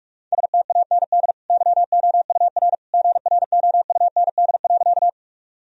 Morse code, heard as a keyed tone, about 46 wpm, 700 Hz. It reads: STAND YOUR GROUND.
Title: Sentences from Top 1000 Words - Review Part 2 - 45wpm